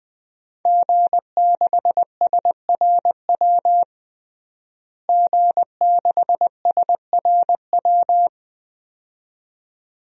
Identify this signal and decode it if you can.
Morse code, heard as a keyed tone, about 20 wpm, 700 Hz. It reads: G6SRW G6SRW
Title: DXCC-100 Call Signs - 5 Characters - Repeated 2x - 20wpm